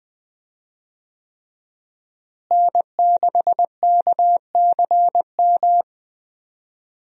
Morse code, heard as a keyed tone, about 20 wpm, 700 Hz. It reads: N6KCM
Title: USA Call Signs - 5 Characters - Repeated 3x - 20wpm